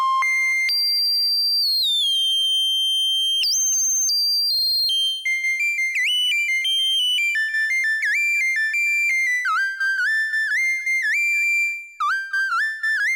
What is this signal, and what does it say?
Dial-up modem handshake.